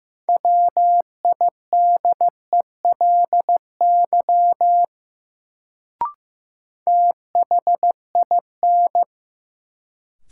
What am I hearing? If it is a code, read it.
Morse code, heard as a keyed tone, about 15 wpm, 700 Hz. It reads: WIDELY THIN
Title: Sentences from Top 1200 Words - New Word Drill Alt - 15wpm